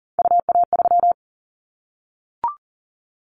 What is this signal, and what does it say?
Morse code, heard as a keyed tone, about 40 wpm, 700 Hz. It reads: UA3
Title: Callsign Prefixes for Top 50 DXCC - 40wpm